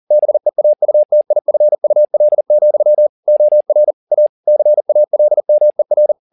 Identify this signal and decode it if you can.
Morse code, heard as a keyed tone, about 40 wpm, 600 Hz. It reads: BEAUTIFUL, OR A CALMER
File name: the old man and the sea 2 40wpm